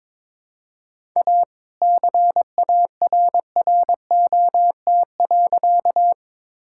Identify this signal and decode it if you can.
Morse code, heard as a keyed tone, about 22 wpm, 700 Hz. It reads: A CARROT.
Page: morse-code-ninja-practice-22wpm